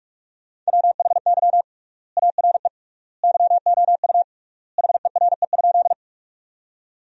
Morse code, heard as a keyed tone, about 46 wpm, 700 Hz. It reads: WHY ARE YOU HERE?
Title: 3x Word-Spacing - Sentences from Top 200 Words - 45wpm